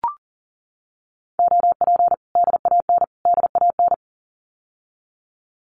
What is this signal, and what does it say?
Morse code, heard as a keyed tone, about 40 wpm, 700 Hz. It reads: OP DAN DAN
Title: Common QSO Phrases Encore 40wpm